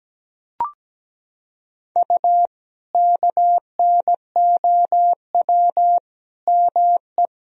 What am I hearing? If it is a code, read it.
Morse code, heard as a keyed tone, about 17 wpm, 700 Hz. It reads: U KNOW ME
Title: Sentences from Top 400 Words and Abbreviations 17wpm